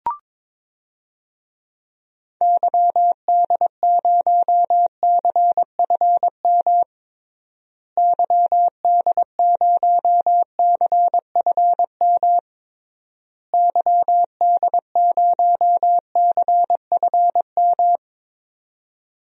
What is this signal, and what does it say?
Morse code, heard as a keyed tone, about 21 wpm, 700 Hz. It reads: YD0CFM YD0CFM YD0CFM
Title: DXCC-100 Call Signs - 6 Characters - Repeated 3x - 22wpm